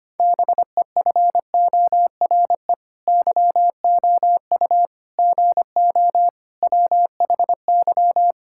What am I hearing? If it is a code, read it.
Morse code, heard as a keyed tone, about 25 wpm, 700 Hz. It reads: BEFORE YOU GO WHY
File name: Sets of 9 Words - 25wpm